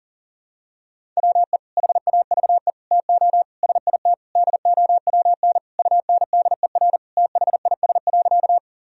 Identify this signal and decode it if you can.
Morse code, heard as a keyed tone, about 40 wpm, 700 Hz. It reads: WE HAVE TO SIT DOWN UNDER THIS.
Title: Sentences from Top 1000 Words - Review Part 1 - 40wpm